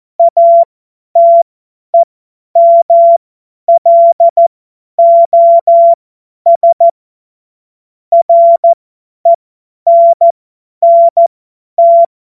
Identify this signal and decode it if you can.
Morse code, heard as a keyed tone, about 14 wpm, spaced as for 10 wpm, 680 Hz. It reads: ATEMLOS RENNT
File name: Morsekurs / Witze / Witz 14. Februar 2023 50 F70